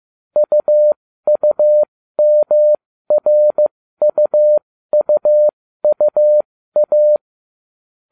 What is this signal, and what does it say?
Morse code, heard as a keyed tone, about 15 wpm, spaced as for 13 wpm, 600 Hz. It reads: UUMRUUUA